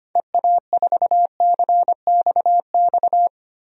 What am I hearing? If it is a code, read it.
Morse code, heard as a keyed tone, about 25 wpm, 700 Hz. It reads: EA4CXX
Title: DXCC-100 Call Signs - 6 Characters - Repeated 3x - 25wpm